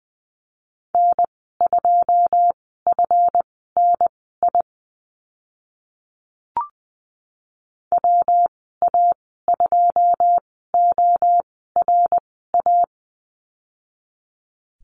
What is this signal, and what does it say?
Morse code, heard as a keyed tone, about 19 wpm, spaced as for 14 wpm, 700 Hz. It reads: N2FNI WA2ORA
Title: 167 Lesson 27g - Number 7 - Call Signs 20wpm